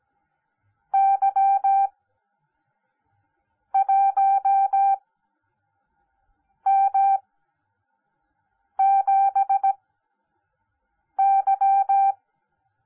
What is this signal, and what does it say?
Morse code, heard as a keyed tone, about 17 wpm, spaced as for 5 wpm, 780 Hz. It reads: Y1M7Y